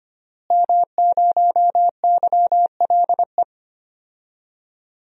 Morse code, heard as a keyed tone, about 25 wpm, 700 Hz. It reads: M0YLE
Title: DXCC-100 Call Signs - 5 Characters - Repeated 2x - 25wpm